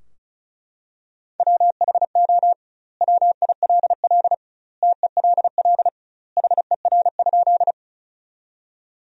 Morse code, heard as a keyed tone, about 36 wpm, 700 Hz. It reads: WHO WILL TELL HER?
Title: 2x Word-Spacing - Sentences from Top 200 Words - 35wpm